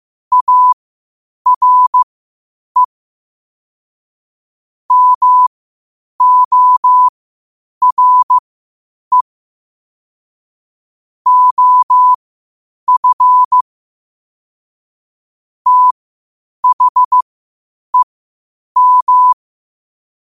Morse code, heard as a keyed tone, about 15 wpm, spaced as for 8 wpm, 1000 Hz. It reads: ARE MORE OF THEM
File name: qotdfiles / 8wpm